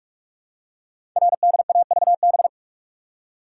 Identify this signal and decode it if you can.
Morse code, heard as a keyed tone, about 44 wpm, 700 Hz. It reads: RDAVB
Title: Code Groups - 5 Letters - 45wpm